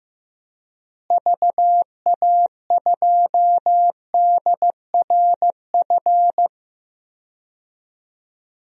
Morse code, heard as a keyed tone, about 15 wpm, 700 Hz. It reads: VA2DRF
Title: DXCC-100 Call Signs - 6 Characters - Repeated 3x - 15wpm